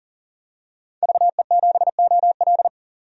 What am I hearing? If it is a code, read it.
Morse code, heard as a keyed tone, about 38 wpm, 700 Hz. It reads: VE7OL
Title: DXCC-100 Call Signs - 5 Characters - 40wpm